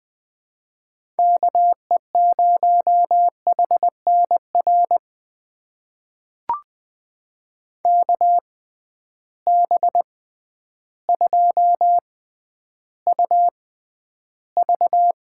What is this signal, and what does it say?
Morse code, heard as a keyed tone, about 20 wpm, spaced as for 11 wpm, 700 Hz. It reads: KE0HNR KB2UV
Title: USA Call Signs - 6 Characters - 6x Character Spacing - 20wpm